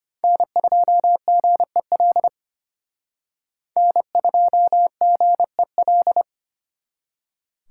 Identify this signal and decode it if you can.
Morse code, heard as a keyed tone, about 29 wpm, 700 Hz. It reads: N2GEL N2GEL
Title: Speed Racing - USA 1x3 Call Signs 25wpm